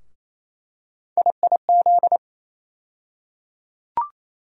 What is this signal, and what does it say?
Morse code, heard as a keyed tone, about 28 wpm, 700 Hz. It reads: IIZ
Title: USA Call Sign Postfixes - 3 Letters 28wpm